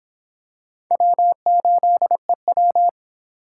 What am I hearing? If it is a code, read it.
Morse code, heard as a keyed tone, about 25 wpm, 700 Hz. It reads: W8EW